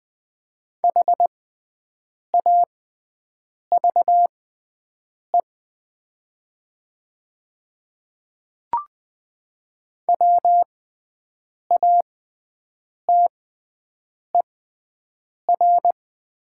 Morse code, heard as a keyed tone, about 20 wpm, spaced as for 7 wpm, 700 Hz. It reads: HAVE WATER
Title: 6x Character Spacing - Top 100 Words - 20wpm